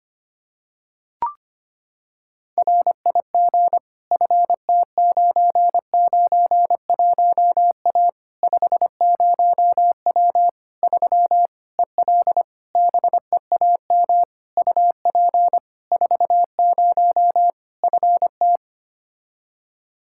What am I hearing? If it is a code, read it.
Morse code, heard as a keyed tone, about 25 wpm, 700 Hz. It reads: RIG FT991A 50W 3 EL BEAM UP 40 FT
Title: Practice QSO - Exchange 2 - Info - 25wpm